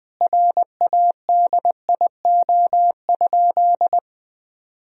Morse code, heard as a keyed tone, about 20 wpm, 700 Hz. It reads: RADIO?
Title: Sentences from Top 1000 Words - Review Part 3 - 20wpm